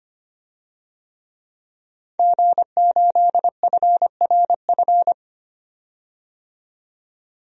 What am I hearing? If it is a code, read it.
Morse code, heard as a keyed tone, about 25 wpm, 700 Hz. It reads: G8FRF
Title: DXCC-100 Call Signs - 5 Characters - 25wpm